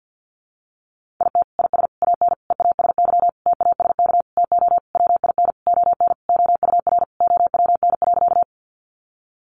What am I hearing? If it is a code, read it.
Morse code, heard as a keyed tone, about 50 wpm, 700 Hz. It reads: IT IS AN EASY TASK TO WIN ON OUR OWN.
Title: Sentences from Top 1500 Words 50wpm